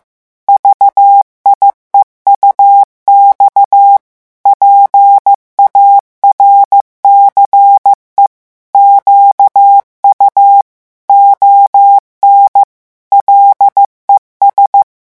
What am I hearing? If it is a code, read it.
Morse code, heard as a keyed tone, about 15 wpm, 800 Hz. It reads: VIEUX PARCE QU ON LES
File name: morse mp3 / revision9 citations humoristiques vitesse 14 mots minute